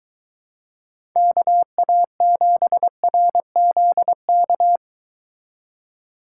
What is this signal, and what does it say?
Morse code, heard as a keyed tone, about 22 wpm, 700 Hz. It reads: KA7RZK